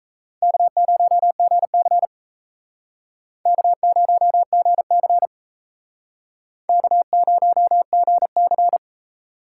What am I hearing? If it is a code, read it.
Morse code, heard as a keyed tone, about 38 wpm, 700 Hz. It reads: K0GC K0GC K0GC